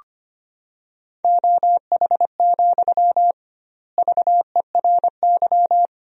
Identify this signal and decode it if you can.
Morse code, heard as a keyed tone, about 25 wpm, 700 Hz. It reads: OH, VERY
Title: 2x Word-Spacing - Sentences from Top 500 Words - 25wpm